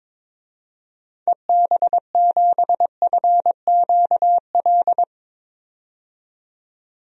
Morse code, heard as a keyed tone, about 21 wpm, 700 Hz. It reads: EB7FQL
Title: DXCC-100 Call Signs - 6 Characters - 22wpm